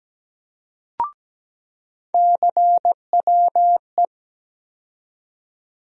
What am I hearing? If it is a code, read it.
Morse code, heard as a keyed tone, about 17 wpm, 700 Hz. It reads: CWE
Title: USA Call Sign Postfixes - 3 Letters 17wpm